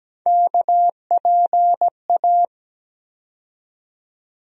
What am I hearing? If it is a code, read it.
Morse code, heard as a keyed tone, about 17 wpm, 700 Hz. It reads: KPA
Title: USA Call Sign Postfixes - 3 Letters 17wpm